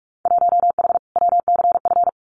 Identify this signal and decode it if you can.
Morse code, heard as a keyed tone, about 44 wpm, 700 Hz. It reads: 1H WCF
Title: Winter Field Day Exchanges 45wpm